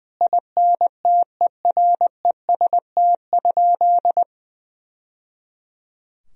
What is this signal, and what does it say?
Morse code, heard as a keyed tone, about 20 wpm, 700 Hz. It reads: INTEREST?
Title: Sentences from Top 1000 Words - Review Part 2 - 20wpm